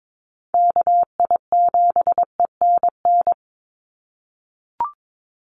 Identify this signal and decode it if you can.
Morse code, heard as a keyed tone, about 22 wpm, 700 Hz. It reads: KI7ENN
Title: USA Call Signs 22wpm